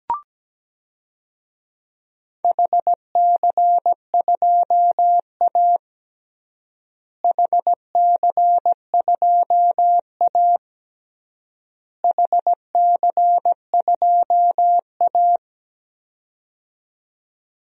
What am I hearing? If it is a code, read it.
Morse code, heard as a keyed tone, about 17 wpm, 700 Hz. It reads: HC2A HC2A HC2A
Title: DXCC-100 Call Signs - 4 Characters - Repeated 3x - 17wpm